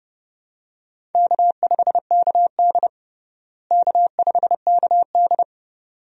Morse code, heard as a keyed tone, about 29 wpm, 700 Hz. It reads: K5KD K5KD
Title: USA Call Signs - 4 Characters - Repeated 2x - 30wpm